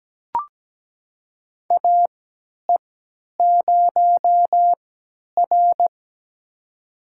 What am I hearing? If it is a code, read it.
Morse code, heard as a keyed tone, about 17 wpm, spaced as for 10 wpm, 700 Hz. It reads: AE0R
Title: USA Call Signs - 4 Characters - 3x Character Spacing - 17wpm